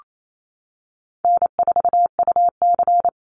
Morse code, heard as a keyed tone, about 28 wpm, 700 Hz. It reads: N4UC
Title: Call Signs - 28wpm